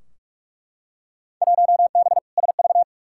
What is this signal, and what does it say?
Morse code, heard as a keyed tone, about 42 wpm, 700 Hz. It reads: 1B SV
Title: Field Day Exchanges Encore - 45wpm